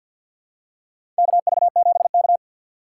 Morse code, heard as a keyed tone, about 47 wpm, 700 Hz. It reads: KV7K
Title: USA Call Signs - 4 Characters - 3x Character Spacing - 50wpm